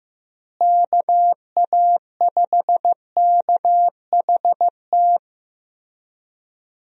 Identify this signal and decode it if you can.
Morse code, heard as a keyed tone, about 15 wpm, 700 Hz. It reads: KA5KHT
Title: USA Call Signs - 6 Characters - Repeated 3x - 15wpm